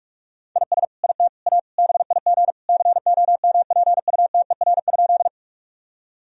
Morse code, heard as a keyed tone, about 46 wpm, 700 Hz. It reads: IS IT A BIG COMPUTER?
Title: Sentences from Top 1000 Words - Review Part 3 - 45wpm